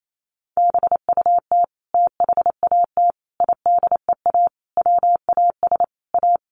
Morse code, heard as a keyed tone, about 28 wpm, 700 Hz. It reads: BUT THAT IDEA WAS A
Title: Sentences from Top 900 Words 28wpm